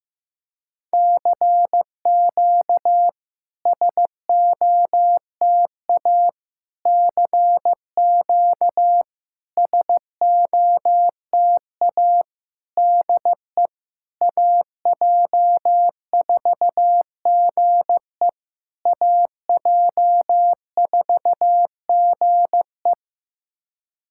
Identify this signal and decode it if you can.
Morse code, heard as a keyed tone, about 15 wpm, 700 Hz. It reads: CQ SOTA CQ SOTA DE AJ4GE AJ4GE
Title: SOTA QSO Practice 15wpm